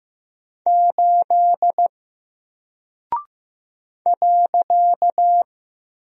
Morse code, heard as a keyed tone, about 15 wpm, 700 Hz. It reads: 8 .
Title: Single Character (Letter:Number:Punctuation:Prosign) Alt 15wpm